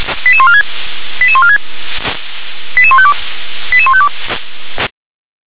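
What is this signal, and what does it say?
Five-tone selective call (ZVEI), heard as quick runs of five tones, tones of about 70 ms. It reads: 88146 88146 88152 88152
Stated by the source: Received by SDR; ZVEI five-tone call.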